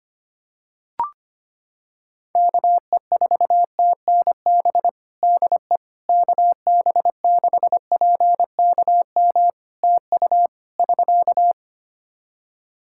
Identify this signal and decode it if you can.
Morse code, heard as a keyed tone, about 25 wpm, 700 Hz. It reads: KE4TNB DE KB6PKM TU <SK>
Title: Practice QSO - Exchange 4 - Ending - 25wpm